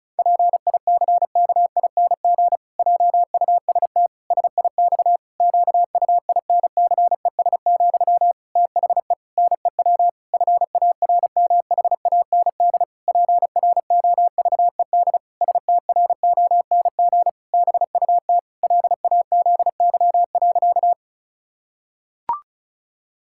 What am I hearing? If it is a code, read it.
Morse code, heard as a keyed tone, about 34 wpm, 700 Hz. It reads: PICKING JUST SIX QUINCES, THE NEW FARMHAND PROVED STRONG BUT LAZY.
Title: Pangram Challenge v2 - 35wpm